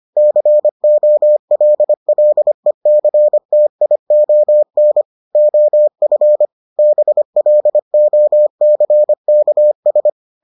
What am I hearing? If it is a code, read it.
Morse code, heard as a keyed tone, about 25 wpm, 600 Hz. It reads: COLLECTION OF BLOCKS